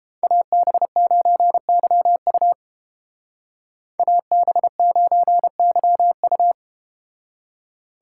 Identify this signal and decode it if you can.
Morse code, heard as a keyed tone, about 31 wpm, 700 Hz. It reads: AB9YU AB9YU